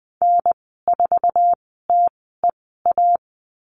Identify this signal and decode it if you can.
Morse code, heard as a keyed tone, about 20 wpm, spaced as for 14 wpm, 700 Hz. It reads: N4TEA
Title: 062 Lesson 11g - Letter h - Call Signs 20wpm